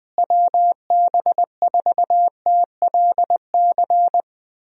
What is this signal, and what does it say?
Morse code, heard as a keyed tone, about 20 wpm, 700 Hz. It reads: WB4TLC
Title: USA Call Signs - 6 Characters - Repeated 2x - 20wpm